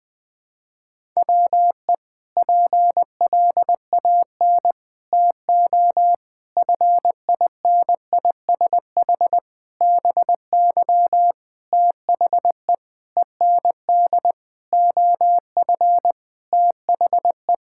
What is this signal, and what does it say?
Morse code, heard as a keyed tone, about 21 wpm, 700 Hz. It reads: WE PLAN TO FINISH BY THE END OF THE
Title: Sentences with 5-Word Phrases - Spoken Beginning - 20wpm